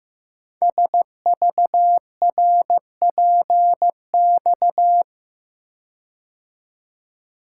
Morse code, heard as a keyed tone, about 15 wpm, 700 Hz. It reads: SVRPX
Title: Code Groups - 5 Letters - 15wpm